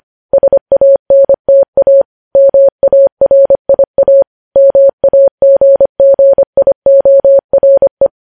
Morse code, heard as a keyed tone, about 25 wpm, 550 Hz. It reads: SANTA MARIA MAGGIORE
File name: podcast / News headlines-1x-25x25